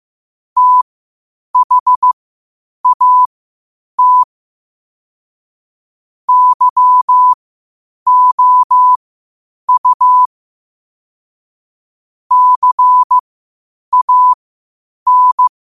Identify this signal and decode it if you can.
Morse code, heard as a keyed tone, about 15 wpm, spaced as for 8 wpm, 1000 Hz. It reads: THAT YOU CAN
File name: qotdfiles / 8wpm